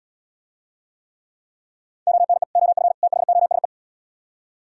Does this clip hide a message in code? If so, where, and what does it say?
Yes: Morse code from 2.1 to 3.6 s, heard as a keyed tone, about 77 wpm, 700 Hz. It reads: ONE OF THOSE